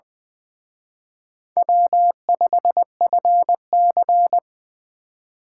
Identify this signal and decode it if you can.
Morse code, heard as a keyed tone, about 23 wpm, 700 Hz. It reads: W5FC
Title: Speed Racing - USA 1x2 Call Signs 20wpm